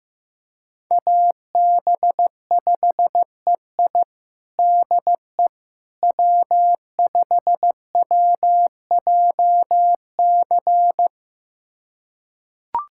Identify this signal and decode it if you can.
Morse code, heard as a keyed tone, about 15 wpm, 700 Hz. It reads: AB5EI DE W5WJC
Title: USA Callsign DE Callsign - 5 Characters - 15wpm